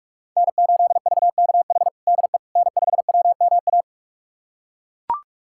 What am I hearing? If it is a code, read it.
Morse code, heard as a keyed tone, about 42 wpm, 700 Hz. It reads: N8VKH DE N5WMA
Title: USA Callsign DE Callsign - 5 Characters - 45wpm